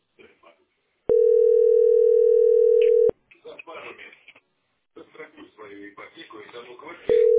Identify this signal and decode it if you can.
Ringback tone (North America), 440 + 480 Hz, with speech.
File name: 6e -e72c-4de3-a55f